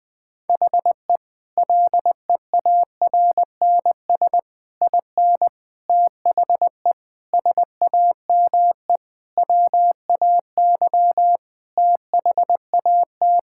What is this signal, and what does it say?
Morse code, heard as a keyed tone, about 20 wpm, 700 Hz. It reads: HE LEARNS IN THE SAME WAY THAT
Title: Sentences with 5-Word Phrases - Spoken Beginning - 20wpm